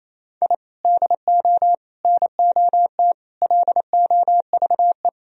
Morse code, heard as a keyed tone, about 28 wpm, 700 Hz. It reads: I DO NOT LOVE
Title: Sentences from Top 1000 Words - Review Part 4 - 28wpm